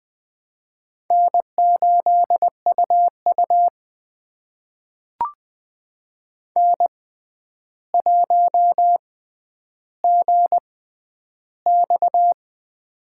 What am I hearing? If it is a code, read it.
Morse code, heard as a keyed tone, about 21 wpm, spaced as for 10 wpm, 700 Hz. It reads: N8UU N1GX
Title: USA Call Signs - 4 Characters - 6x Character Spacing - 20wpm